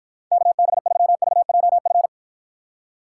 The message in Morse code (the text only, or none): KB3VJF